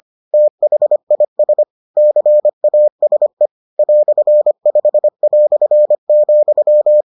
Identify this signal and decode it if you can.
Morse code, heard as a keyed tone, about 25 wpm, 600 Hz. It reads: THIS CASE "5",